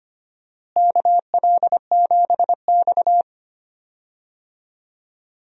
Morse code, heard as a keyed tone, about 25 wpm, 700 Hz. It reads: KL7X
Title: DXCC-100 Call Signs - 4 Characters - Repeated 2x - 25wpm